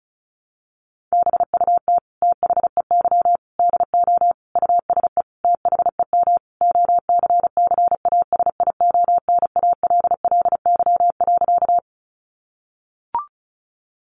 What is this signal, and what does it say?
Morse code, heard as a keyed tone, about 34 wpm, 700 Hz. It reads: BUT THEY DO USE THEM OCCASIONALLY.
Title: Sentences from Top 1300 Words 35wpm